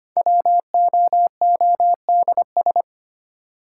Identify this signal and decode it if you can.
Morse code, heard as a keyed tone, about 25 wpm, 700 Hz. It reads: WOODS
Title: Sentences from Top 1000 Words - Review Part 2 - New Word Drill - 25wpm